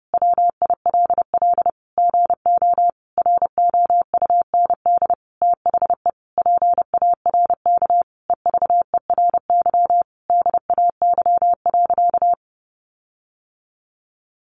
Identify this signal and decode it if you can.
Morse code, heard as a keyed tone, about 30 wpm, 700 Hz. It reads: WILL GO ROUND THE PARK EVERY DAY.
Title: Sentences from Top 1300 Words 30wpm